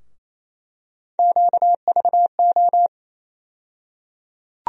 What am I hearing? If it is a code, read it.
Morse code, heard as a keyed tone, about 28 wpm, 700 Hz. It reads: QVO
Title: USA Call Sign Postfixes - 3 Letters 28wpm